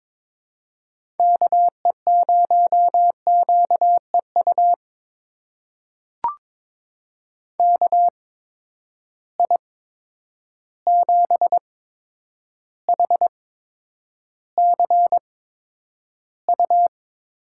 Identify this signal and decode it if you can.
Morse code, heard as a keyed tone, about 22 wpm, spaced as for 9 wpm, 700 Hz. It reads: KE0QEU KI7HCU
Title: USA Call Signs - 6 Characters - 8x Character Spacing - 22wpm